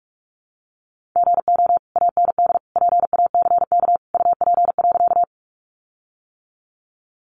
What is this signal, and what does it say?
Morse code, heard as a keyed tone, about 44 wpm, 700 Hz. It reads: GO AND PACK UP.
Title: Sentences from Top 1300 Words 45wpm